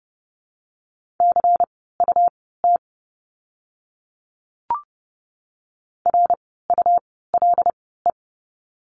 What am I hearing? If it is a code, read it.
Morse code, heard as a keyed tone, about 30 wpm, spaced as for 17 wpm, 700 Hz. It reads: CUT RULE
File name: Lesson 17f - Letter u - Words 30wpm